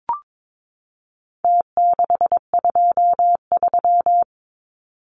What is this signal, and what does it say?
Morse code, heard as a keyed tone, about 22 wpm, 700 Hz. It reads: T623
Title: Numbers - Contest Style - 4-Digit 22wpm